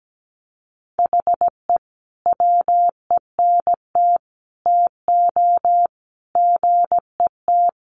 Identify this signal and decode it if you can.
Morse code, heard as a keyed tone, about 17 wpm, 700 Hz. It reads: HE WENT TO GET
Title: Sentences from Top 1500 Words 17wpm